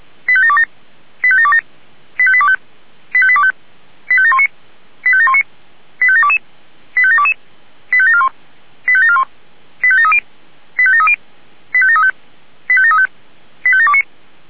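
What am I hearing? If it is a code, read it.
Five-tone selective call (ZVEI), heard as quick runs of five tones, tones of about 80 ms. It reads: 85727 85727 85825 85825 85719 85719 85722 85722 85741 85741 85730 85730 85735 85735 85729
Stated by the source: ZVEI five-tone selective call received on an SDR.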